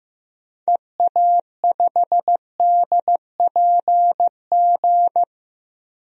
Morse code, heard as a keyed tone, about 15 wpm, 700 Hz. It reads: EA5DPG
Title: DXCC-100 Call Signs - 6 Characters - 15wpm